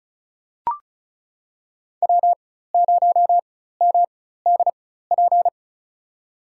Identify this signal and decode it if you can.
Morse code, heard as a keyed tone, about 36 wpm, spaced as for 16 wpm, 700 Hz. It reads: W0MDP
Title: USA Call Signs - 5 Characters - 4x Character Spacing - 35wpm